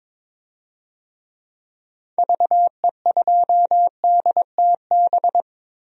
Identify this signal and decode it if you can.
Morse code, heard as a keyed tone, about 22 wpm, 700 Hz. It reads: VE2DTB
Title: DXCC-100 Call Signs - 6 Characters - Repeated 3x - 22wpm